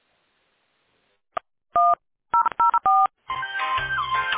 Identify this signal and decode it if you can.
DTMF, heard as telephone touch tones, tones of about 210 ms. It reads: 1004